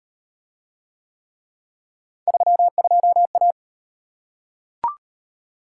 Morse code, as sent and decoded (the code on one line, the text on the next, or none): ...-- ..--- .-
32A